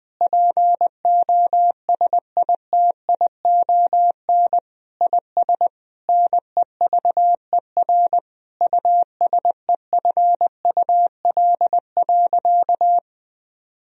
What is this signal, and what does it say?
Morse code, heard as a keyed tone, about 20 wpm, 700 Hz. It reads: POSITION IS NEVER USEFUL.
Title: Sentences from Top 1000 Words - Review Part 4 - 20wpm